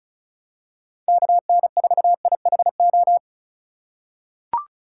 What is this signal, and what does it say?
Morse code, heard as a keyed tone, about 36 wpm, 700 Hz. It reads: KN4IHO
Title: USA Call Signs - 6 Characters - 2x Character Spacing - 35wpm